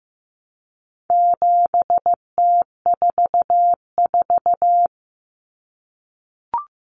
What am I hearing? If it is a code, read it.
Morse code, heard as a keyed tone, about 15 wpm, 700 Hz. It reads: 7T44
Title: Common Cut Numbers - 4-Digit 15wpm